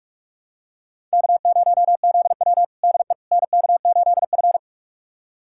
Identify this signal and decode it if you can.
Morse code, heard as a keyed tone, about 46 wpm, 700 Hz. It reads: K0ZW DE NK8F
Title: USA Callsign DE Callsign - 4 Characters - 45wpm